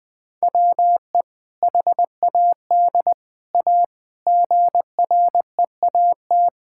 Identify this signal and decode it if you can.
Morse code, heard as a keyed tone, about 20 wpm, 700 Hz. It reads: WE HAD A GREAT
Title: Sentences with 2-Word Phrases - Spoken Beginning - 20wpm